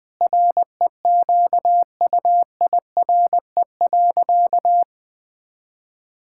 Morse code, heard as a keyed tone, about 20 wpm, 700 Hz. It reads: REQUIRE.
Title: Sentences from Top 1000 Words - Review Part 6 - 20wpm